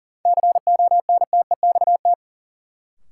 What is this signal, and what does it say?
Morse code, heard as a keyed tone, about 40 wpm, 700 Hz. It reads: CONTEXT
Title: Sentences from Top 1000 Words - Review Part 7 - New Word Drill - 40wpm